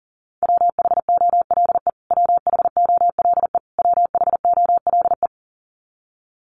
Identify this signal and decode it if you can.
Morse code, heard as a keyed tone, about 40 wpm, 700 Hz. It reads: WHOLE WHOLE WHOLE
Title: Top 500 Words Repeated 40wpm